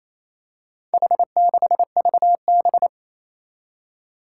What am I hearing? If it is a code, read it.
Morse code, heard as a keyed tone, about 28 wpm, 700 Hz. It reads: H6VB
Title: Four Challenging Letter-Number Combinations 28wpm